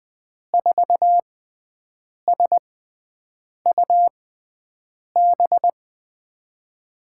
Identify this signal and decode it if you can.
Morse code, heard as a keyed tone, about 20 wpm, spaced as for 7 wpm, 700 Hz. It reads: 4SUB
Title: USA Call Signs - 5 Characters - 6x Character Spacing - 20wpm